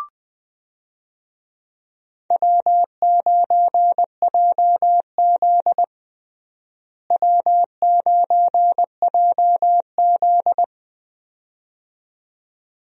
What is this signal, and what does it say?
Morse code, heard as a keyed tone, about 21 wpm, 700 Hz. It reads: W9JZ W9JZ
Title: USA Call Signs - 4 Characters - Repeated 2x - 20wpm